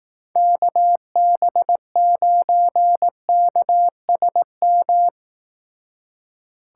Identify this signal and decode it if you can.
Morse code, heard as a keyed tone, about 17 wpm, 700 Hz. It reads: KB9KSM
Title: Speed Racing - USA 2x3 Call Signs 15wpm